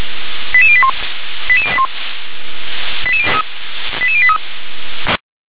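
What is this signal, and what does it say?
Five-tone selective call (ZVEI), heard as quick runs of five tones, tones of about 70 ms. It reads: 88081 88081 88083
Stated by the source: ZVEI five-tone selective call received by SDR.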